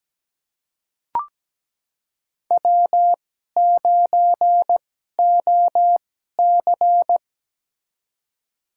Morse code, heard as a keyed tone, about 17 wpm, spaced as for 12 wpm, 700 Hz. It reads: W9OC
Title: USA Call Signs - 4 Characters - 2x Character Spacing - 17wpm